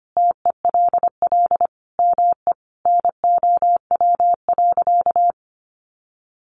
Morse code, heard as a keyed tone, about 25 wpm, 700 Hz. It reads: TELL ME NOW.
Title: Sentences from Top 1100 Words 25wpm